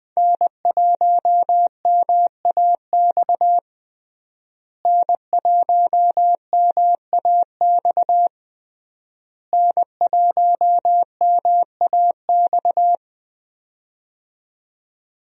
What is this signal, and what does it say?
Morse code, heard as a keyed tone, about 20 wpm, 700 Hz. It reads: N1MAX N1MAX N1MAX
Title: USA Call Signs - 5 Characters - Repeated 3x - 20wpm